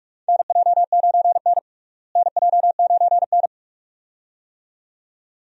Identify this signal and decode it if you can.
Morse code, heard as a keyed tone, about 46 wpm, 700 Hz. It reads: NJ9N NJ9N
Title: USA Call Signs - 4 Characters - Repeated 2x - 45wpm